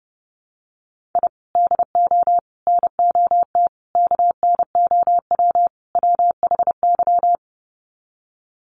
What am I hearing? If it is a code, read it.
Morse code, heard as a keyed tone, about 30 wpm, 700 Hz. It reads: I DO NOT KNOW WHY
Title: Sets of 5 Words 30wpm